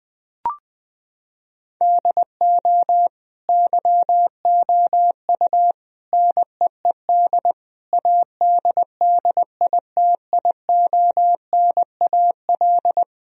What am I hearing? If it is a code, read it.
Morse code, heard as a keyed tone, about 21 wpm, 700 Hz. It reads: DO YOU NEED ADDITIONAL
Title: Sentences from Top 1000 Words - Review Part 4 - 20wpm